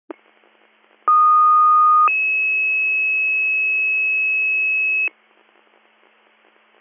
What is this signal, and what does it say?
Two-tone page, heard as a one-second tone, then a three-second tone, about 1230 then 2600 Hz. It is heard from 1.1 to 5.1 s.